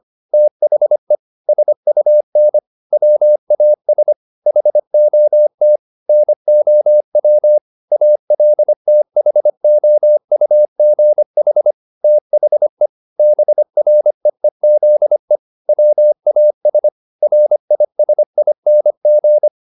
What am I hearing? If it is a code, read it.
Morse code, heard as a keyed tone, about 25 wpm, 600 Hz. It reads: THE SUN WAS HOT NOW ALTHOUGH THE BREEZE WAS RISING
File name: the old man and the sea part 1 25wpm